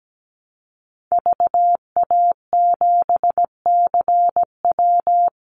Morse code, heard as a keyed tone, about 17 wpm, 700 Hz. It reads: VA7CW
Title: Call Signs Repeated 3x - 17wpm